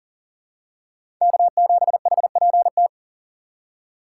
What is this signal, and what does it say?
Morse code, heard as a keyed tone, about 40 wpm, 700 Hz. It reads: K7HPT
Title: K1USN SST QSO Practice - 40wpm